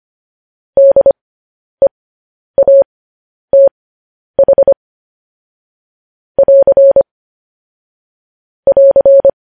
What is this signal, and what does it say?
Morse code, heard as a keyed tone, about 25 wpm, spaced as for 10 wpm, 550 Hz. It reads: DEATH <AR> <AR>